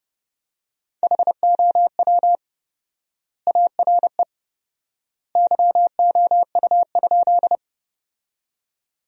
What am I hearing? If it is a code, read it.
Morse code, heard as a keyed tone, about 30 wpm, 700 Hz. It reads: HOW ARE YOU?
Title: 4x Word-Spacing - Sentences from Top 100 Words - 30wpm